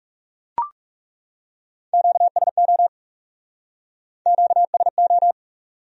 Morse code, heard as a keyed tone, about 39 wpm, 700 Hz. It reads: QSO QSO